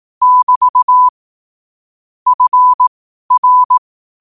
Morse code, heard as a keyed tone, about 18 wpm, spaced as for 12 wpm, 1000 Hz. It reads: <BT> FR
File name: qotdfiles / 12wpm